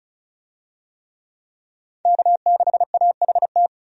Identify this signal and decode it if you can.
Morse code, heard as a keyed tone, about 36 wpm, 700 Hz. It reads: K6AHT
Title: USA Call Signs - 5 Characters - 35wpm